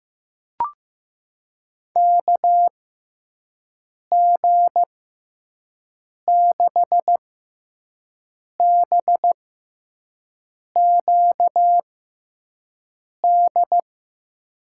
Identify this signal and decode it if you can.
Morse code, heard as a keyed tone, about 15 wpm, spaced as for 5 wpm, 700 Hz. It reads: KG6BQD
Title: USA Call Signs - 6 Characters - 6x Character Spacing - 15wpm